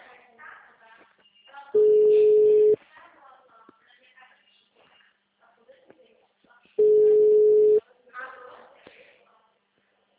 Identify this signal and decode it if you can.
Ringback tone (Europe, 425 Hz), with speech.